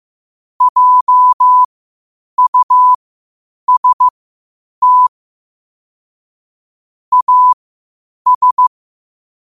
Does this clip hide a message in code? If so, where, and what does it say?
Yes: Morse code from 0.6 to 8.7 s, heard as a keyed tone, about 15 wpm, spaced as for 8 wpm, 1000 Hz. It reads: JUST AS